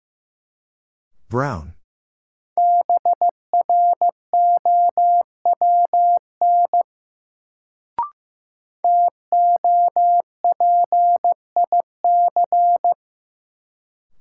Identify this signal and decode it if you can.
Morse code, heard as a keyed tone, about 15 wpm, 700 Hz. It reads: BROWN TOPIC
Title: Sentences from Top 1200 Words - New Word Drill - 15wpm